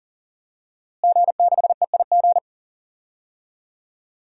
Morse code, heard as a keyed tone, about 38 wpm, 700 Hz. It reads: G6EIG
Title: DXCC-100 Call Signs - 5 Characters - Repeated 3x - 40wpm